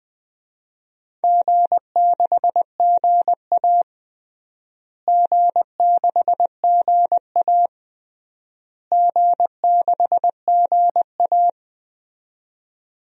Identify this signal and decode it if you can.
Morse code, heard as a keyed tone, about 20 wpm, 700 Hz. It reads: G6GA G6GA G6GA
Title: DXCC-100 Call Signs - 4 Characters - Repeated 3x - 20wpm